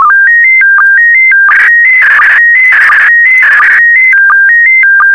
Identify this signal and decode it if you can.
SSTV (slow-scan television), PD 90.
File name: sstv / PD90-VK4MSL